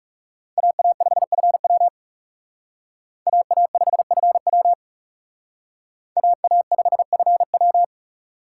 Morse code, heard as a keyed tone, about 38 wpm, 700 Hz. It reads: AA5FW AA5FW AA5FW